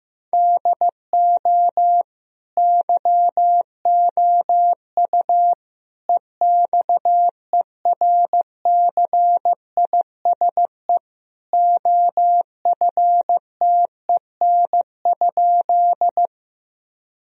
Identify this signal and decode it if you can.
Morse code, heard as a keyed tone, about 15 wpm, 700 Hz. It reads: DO YOU EXERCISE OFTEN?
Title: Sentences from Top 1000 Words - Review Part 4 - 15wpm